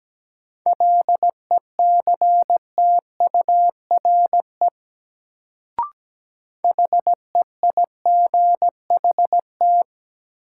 Morse code, heard as a keyed tone, about 17 wpm, 700 Hz. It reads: LECTURE HEIGHT
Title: Sentences from Top 1600 Words - New Word Drill Alt - 17wpm